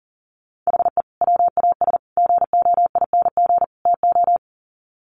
Morse code, heard as a keyed tone, about 40 wpm, 700 Hz. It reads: HE WAS GOING TO
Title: Sets of 4 Words 40wpm